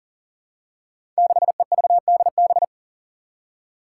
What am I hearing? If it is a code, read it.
Morse code, heard as a keyed tone, about 40 wpm, 700 Hz. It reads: 6EVDB
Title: Five Challenging Letter-Number Combinations 40wpm